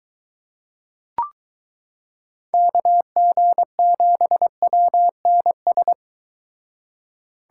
Morse code, heard as a keyed tone, about 23 wpm, 700 Hz. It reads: KG7WNS